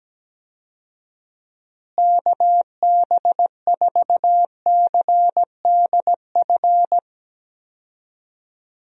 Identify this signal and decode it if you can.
Morse code, heard as a keyed tone, about 17 wpm, 700 Hz. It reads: KB4CDF
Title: USA Call Signs - 6 Characters - Repeated 2x - 17wpm